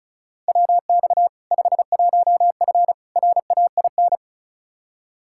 Morse code, heard as a keyed tone, about 36 wpm, 700 Hz. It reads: WX 51F RAIN
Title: K1USN SST QSO Practice 35wpm